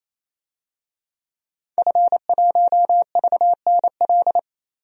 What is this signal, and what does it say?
Morse code, heard as a keyed tone, about 28 wpm, 700 Hz. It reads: F1VNL
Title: DXCC-100 Call Signs - 5 Characters - 28wpm